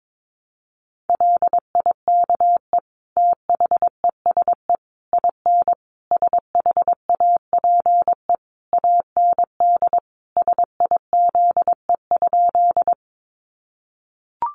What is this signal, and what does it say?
Morse code, heard as a keyed tone, about 22 wpm, 700 Hz. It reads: LIKE THESE IN SHAPE AND SIZE?
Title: Sentences from Top 500 Words 22wpm